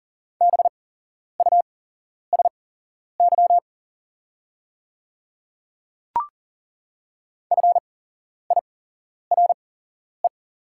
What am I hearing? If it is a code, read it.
Morse code, heard as a keyed tone, about 40 wpm, spaced as for 11 wpm, 690 Hz. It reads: BUSY FIRE
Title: 8x Character Spacing - Top 500 Words - 40wpm